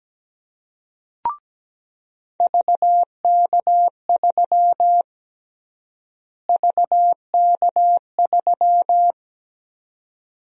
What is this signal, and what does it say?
Morse code, heard as a keyed tone, about 17 wpm, 700 Hz. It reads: VK3 VK3
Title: USA Call Sign Prefixes - Repeated 2x - 17wpm